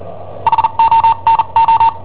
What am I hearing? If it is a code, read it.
Morse code, heard as a keyed tone, about 39 wpm, spaced as for 33 wpm, 910 Hz. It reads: HONO
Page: Cassetta esercizi preparata da I7OHP per IW5ELP